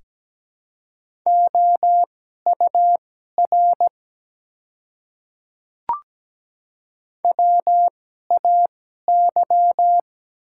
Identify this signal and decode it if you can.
Morse code, heard as a keyed tone, about 17 wpm, spaced as for 12 wpm, 690 Hz. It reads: OUR WAY
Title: 2x Character Spacing - Top 300 Words - 17wpm